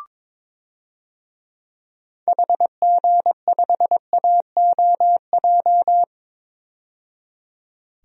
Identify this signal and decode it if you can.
Morse code, heard as a keyed tone, about 22 wpm, 700 Hz. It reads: HG5AOJ
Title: DXCC-100 Call Signs - 6 Characters - 22wpm